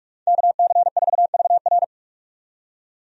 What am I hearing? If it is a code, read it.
Morse code, heard as a keyed tone, about 46 wpm, 700 Hz. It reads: KC4VR
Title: USA Call Signs - 5 Characters - 45wpm